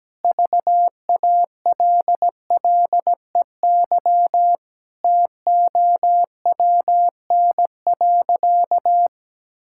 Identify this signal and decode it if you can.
Morse code, heard as a keyed tone, about 17 wpm, 700 Hz. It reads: VALLEY TOWN.
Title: Pangram Challenge v2 - 17wpm